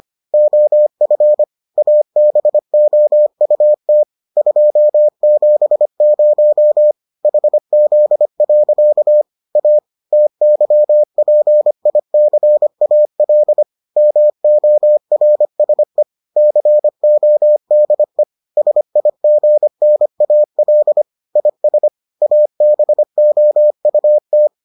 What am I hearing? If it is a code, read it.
Morse code, heard as a keyed tone, about 25 wpm, 600 Hz. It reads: OF ABOUT 270 HZ. A TYPICAL MORSE CODE SIGNAL IS ABOUT